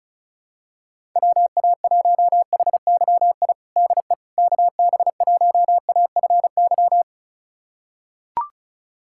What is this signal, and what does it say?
Morse code, heard as a keyed tone, about 34 wpm, 700 Hz. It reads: WA1HYI DE KB1AFY
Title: USA Callsign DE Callsign - 6 Characters - 35wpm